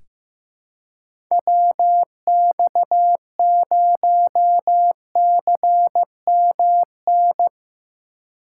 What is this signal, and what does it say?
Morse code, heard as a keyed tone, about 15 wpm, 700 Hz. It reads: WX0CMN
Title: USA Call Signs - 6 Characters - 4x Character Spacing - 15wpm